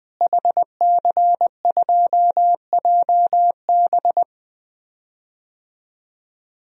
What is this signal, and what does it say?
Morse code, heard as a keyed tone, about 20 wpm, 700 Hz. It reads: HC2JB
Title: DXCC-100 Call Signs - 5 Characters - Repeated 2x - 20wpm